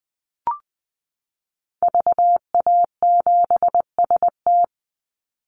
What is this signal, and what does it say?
Morse code, heard as a keyed tone, about 20 wpm, 700 Hz. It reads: VA7ST
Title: Call Signs Repeated 3x - 20wpm